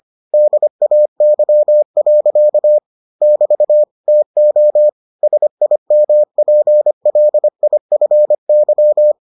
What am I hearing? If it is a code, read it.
Morse code, heard as a keyed tone, about 25 wpm, 600 Hz. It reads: DAY. <BT> TO SIMPLIFY